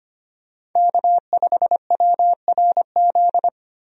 Morse code, heard as a keyed tone, about 25 wpm, 700 Hz. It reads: K5WRZ
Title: USA Call Signs - 5 Characters - 25wpm